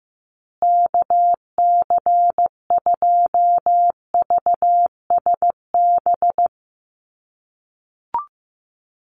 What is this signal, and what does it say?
Morse code, heard as a keyed tone, about 15 wpm, 700 Hz. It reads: KC2VSB
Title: USA Call Signs Repeated 3x 15wpm